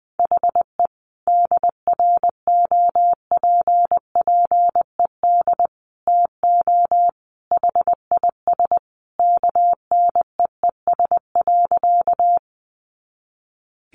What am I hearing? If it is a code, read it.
Morse code, heard as a keyed tone, about 20 wpm, 700 Hz. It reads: HE DROPPED TO HIS KNEES.
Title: A Sound Of Thunder 20wpm